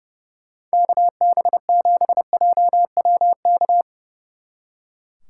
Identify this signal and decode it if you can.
Morse code, heard as a keyed tone, about 37 wpm, 700 Hz. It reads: KB7JWK
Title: Speed Racing - USA 2x3 Call Signs 30wpm